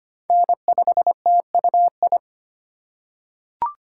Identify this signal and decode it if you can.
Morse code, heard as a keyed tone, about 25 wpm, 700 Hz. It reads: N5TUI
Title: USA Call Signs - 2x Character Spacing - 25wpm